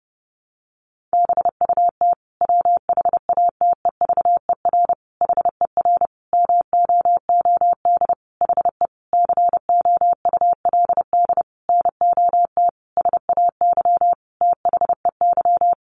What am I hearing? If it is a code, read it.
Morse code, heard as a keyed tone, about 30 wpm, 700 Hz. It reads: BUT WHATEVER HER MOOD HE COULD NOT SAY THEY
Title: Sentences from Top 1100 Words 30wpm